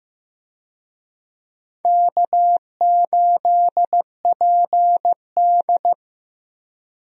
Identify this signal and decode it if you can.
Morse code, heard as a keyed tone, about 15 wpm, 700 Hz. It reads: K8PD
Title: USA Call Signs - 4 Characters - Repeated 3x - 15wpm